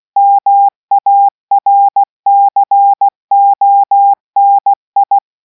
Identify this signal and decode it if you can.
Morse code, heard as a keyed tone, about 16 wpm, 800 Hz. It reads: MARCONI